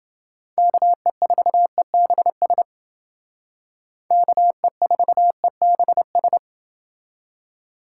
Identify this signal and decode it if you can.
Morse code, heard as a keyed tone, about 28 wpm, 700 Hz. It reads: KE4EBS KE4EBS